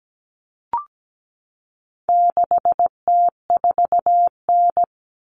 Morse code, heard as a keyed tone, about 17 wpm, 700 Hz. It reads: 6T4N
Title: Common Cut Numbers - 4-Digit 17wpm